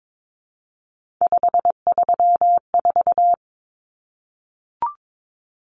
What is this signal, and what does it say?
Morse code, heard as a keyed tone, about 22 wpm, 700 Hz. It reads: 534
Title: Common Cut Numbers - 3-Digit 22wpm